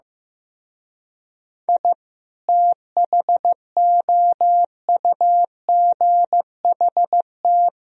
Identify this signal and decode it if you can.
Morse code, heard as a keyed tone, about 15 wpm, 700 Hz. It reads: I THOUGHT
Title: Speed-Racing - Sets of 2 Words 15wpm